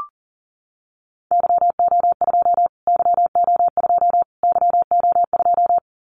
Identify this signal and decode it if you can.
Morse code, heard as a keyed tone, about 40 wpm, 700 Hz. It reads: YO2 YO2 YO2
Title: Callsign Prefixes for Top 25 DXCC - Repeated 3x - 40wpm